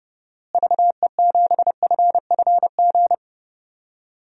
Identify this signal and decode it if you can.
Morse code, heard as a keyed tone, about 30 wpm, 700 Hz. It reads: VE7FFG
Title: DXCC-100 Call Signs - 6 Characters - Repeated 2x - 30wpm